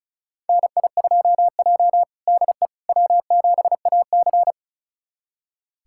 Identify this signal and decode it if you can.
Morse code, heard as a keyed tone, about 34 wpm, 700 Hz. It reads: NI2J DE W7AC
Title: USA Callsign DE Callsign - 4 Characters - 35wpm